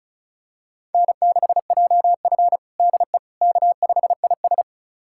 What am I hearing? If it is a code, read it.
Morse code, heard as a keyed tone, about 36 wpm, 690 Hz. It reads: N6JF DE K5IS
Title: USA Callsign DE Callsign - 4 Characters - 35wpm